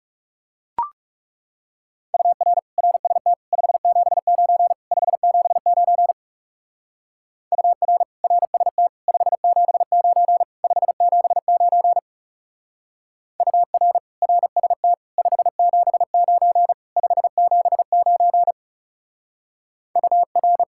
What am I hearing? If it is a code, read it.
Morse code, heard as a keyed tone, about 36 wpm, 700 Hz. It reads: UR RST 579 579 UR RST 579 579 UR RST 579 579 UR
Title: Speed-Racing Common QSO Phrases - 30wpm